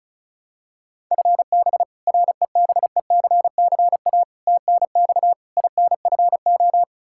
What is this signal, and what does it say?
Morse code, heard as a keyed tone, about 36 wpm, 700 Hz. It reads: FB REBECCA TNX INFO
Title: Practice QSO - Exchange 3 - More Info - 35wpm